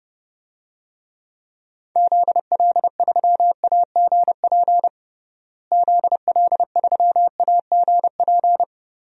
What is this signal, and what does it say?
Morse code, heard as a keyed tone, about 32 wpm, 700 Hz. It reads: ZL3AGP ZL3AGP
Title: DXCC-100 Call Signs - 6 Characters - Repeated 2x - 30wpm